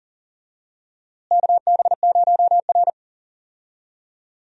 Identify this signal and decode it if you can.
Morse code, heard as a keyed tone, about 40 wpm, 700 Hz. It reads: KB0R
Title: USA Call Signs - 4 Characters - Repeated 3x - 40wpm